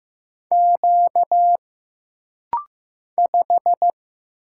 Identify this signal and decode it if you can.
Morse code, heard as a keyed tone, about 15 wpm, 700 Hz. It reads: Q 5
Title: Single Letter-Number Alt - Rapid-Fire 15wpm